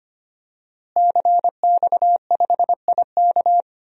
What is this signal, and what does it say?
Morse code, heard as a keyed tone, about 25 wpm, 700 Hz. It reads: CX5IK
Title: DXCC-100 Call Signs - 5 Characters - Repeated 2x - 25wpm